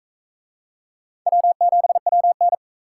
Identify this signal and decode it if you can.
Morse code, heard as a keyed tone, about 39 wpm, 700 Hz. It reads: W7WN